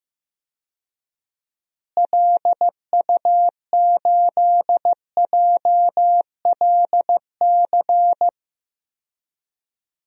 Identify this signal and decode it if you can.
Morse code, heard as a keyed tone, about 15 wpm, 700 Hz. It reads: LU8JLC
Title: DXCC-100 Call Signs - 6 Characters - Repeated 2x - 15wpm